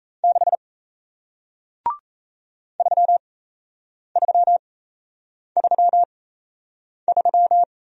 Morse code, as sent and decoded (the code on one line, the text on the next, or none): -.... / ...-- ...-- ...-- ...--
6 3333